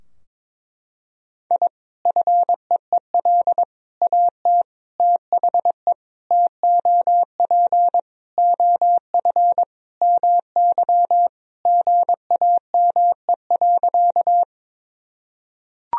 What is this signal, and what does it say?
Morse code, heard as a keyed tone, about 21 wpm, 700 Hz. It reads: I FEEL AT THE TOP OF MY GAME.
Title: Sentences from Top 1000 Words - Review Part 2 - 22wpm